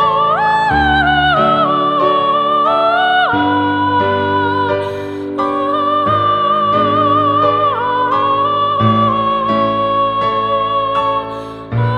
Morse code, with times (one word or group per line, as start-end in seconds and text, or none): none